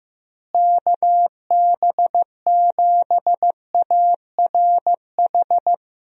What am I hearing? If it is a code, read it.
Morse code, heard as a keyed tone, about 15 wpm, 700 Hz. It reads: KB7ARH
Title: USA Call Signs - 6 Characters - Repeated 3x - 15wpm